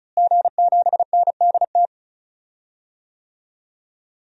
Morse code, heard as a keyed tone, about 34 wpm, 700 Hz. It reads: G7NDT
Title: DXCC-100 Call Signs - 5 Characters - Repeated 2x - 35wpm